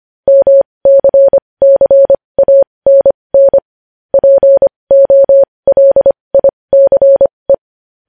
Morse code, heard as a keyed tone, about 25 wpm, spaced as for 20 wpm, 550 Hz. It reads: MCCANN POLICE
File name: podcast / News headlines-1x-25x20